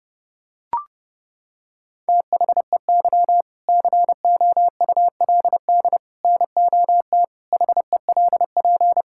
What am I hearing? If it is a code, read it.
Morse code, heard as a keyed tone, about 30 wpm, 700 Hz. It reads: THEY COULD NOT HELP
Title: Sentences from Top 1000 Words - Review Part 5 - 30wpm